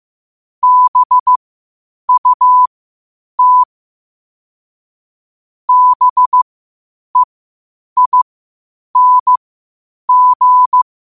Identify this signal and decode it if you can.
Morse code, heard as a keyed tone, about 15 wpm, spaced as for 8 wpm, 1000 Hz. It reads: BUT BEING
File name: qotdfiles / 8wpm